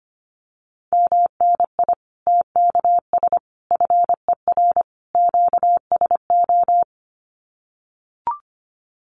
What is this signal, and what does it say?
Morse code, heard as a keyed tone, about 25 wpm, 700 Hz. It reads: MNI TKS FER QSO
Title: Common QSO Phrases 25wpm